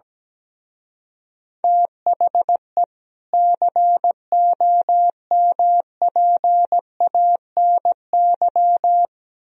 Morse code, heard as a keyed tone, about 17 wpm, 700 Hz. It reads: THE COMPANY
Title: Speed-Racing - Sets of 2 Words 17wpm